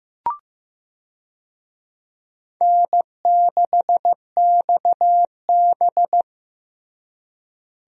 Morse code, heard as a keyed tone, about 15 wpm, 700 Hz. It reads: N6XB
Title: USA Call Signs - 4 Characters - Repeated 2x - 15wpm